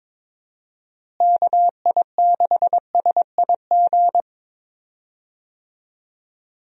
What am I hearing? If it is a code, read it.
Morse code, heard as a keyed tone, about 22 wpm, 700 Hz. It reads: KI6SIG
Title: USA Call Signs - 6 Characters - Repeated 3x - 22wpm